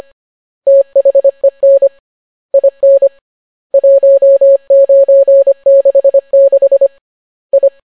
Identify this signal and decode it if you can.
Morse code, heard as a keyed tone, about 25 wpm, 550 Hz. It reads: THEN IN 1966 I